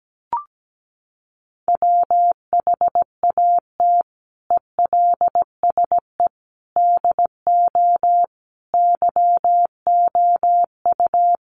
Morse code, heard as a keyed tone, about 17 wpm, 700 Hz. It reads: WHAT ELSE DO YOU
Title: Sentences from Top 1100 Words 17wpm